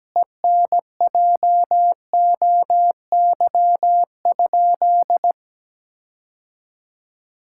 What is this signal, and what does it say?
Morse code, heard as a keyed tone, about 17 wpm, 700 Hz. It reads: ENJOY?
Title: Sentences from Top 1000 Words - Review Part 5 - 17wpm